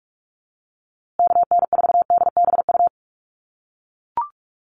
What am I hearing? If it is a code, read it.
Morse code, heard as a keyed tone, about 44 wpm, 700 Hz. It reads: KN4DBU
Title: USA Call Signs Repeated 3x 45wpm